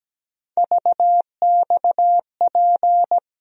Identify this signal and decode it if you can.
Morse code, heard as a keyed tone, about 17 wpm, 700 Hz. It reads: VXP
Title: USA Call Sign Postfixes - 3 Letters 17wpm